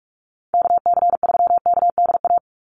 Morse code, heard as a keyed tone, about 44 wpm, 700 Hz. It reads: KC3KDA
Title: USA Call Signs 45wpm